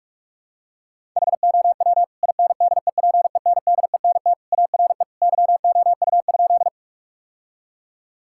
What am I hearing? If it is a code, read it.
Morse code, heard as a keyed tone, about 46 wpm, 700 Hz. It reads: HOW INDEPENDENT ARE YOU?
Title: Sentences from Top 1000 Words - Review Part 6 - 45wpm